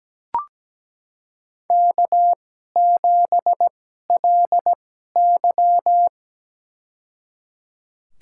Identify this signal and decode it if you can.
Morse code, heard as a keyed tone, about 17 wpm, spaced as for 12 wpm, 700 Hz. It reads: K7LY
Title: USA Call Signs - 4 Characters - 2x Character Spacing - 17wpm